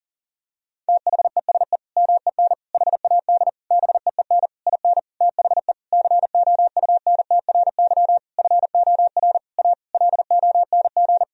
Morse code, heard as a keyed tone, about 40 wpm, 700 Hz. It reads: THESE MEN HAD BEEN IN THE COUNTRY FOR A LONG
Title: Sentences from Top 1000 Words - Review Part 1 - 40wpm